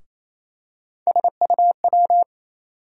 Morse code, heard as a keyed tone, about 28 wpm, 700 Hz. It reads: SUW
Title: USA Call Sign Postfixes - 3 Letters 28wpm